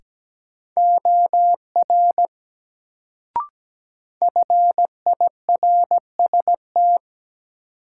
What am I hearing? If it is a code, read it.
Morse code, heard as a keyed tone, about 17 wpm, 700 Hz. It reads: OR FIRST
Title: Sentences from Top 100 Words - New Word Drill - 17wpm